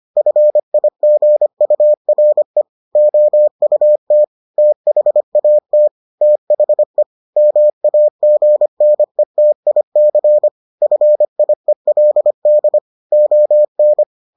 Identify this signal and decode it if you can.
Morse code, heard as a keyed tone, about 25 wpm, 600 Hz. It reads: FIGURE OUT THAT THE MAGNETIC FIELD ON